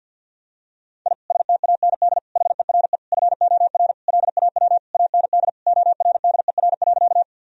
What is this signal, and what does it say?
Morse code, heard as a keyed tone, about 49 wpm, 700 Hz. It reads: I STAND HERE FOR LAW AND ORDER.
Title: Sentences from Top 1000 Words - Review Part 3 - 50wpm